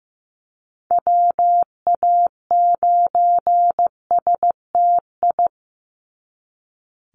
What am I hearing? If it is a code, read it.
Morse code, heard as a keyed tone, about 15 wpm, 700 Hz. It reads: WA9STI
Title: NA SOTA Chaser Callsigns 15wpm